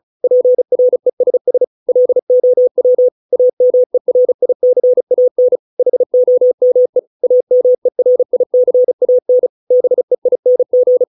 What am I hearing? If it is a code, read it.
Morse code, heard as a keyed tone, about 34 wpm, 500 Hz. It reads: PRESS LOW AMERICAN HOME AMERICAN BEING